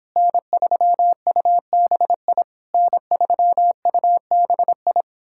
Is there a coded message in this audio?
Yes: Morse code from 0.2 to 5.0 s, heard as a keyed tone, about 29 wpm, 700 Hz. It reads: N3UBI N3UBI